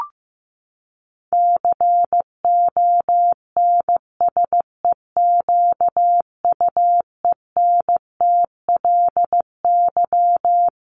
Morse code, heard as a keyed tone, about 15 wpm, 700 Hz. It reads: CONSEQUENTLY
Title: LY Suffix - Top 100 Words with Base Words 15wpm